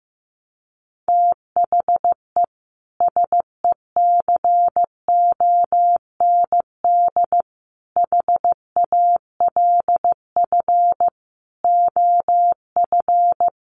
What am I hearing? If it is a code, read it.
Morse code, heard as a keyed tone, about 15 wpm, 700 Hz. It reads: THE SECOND HALF OF
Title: Sets of 5 Words 15wpm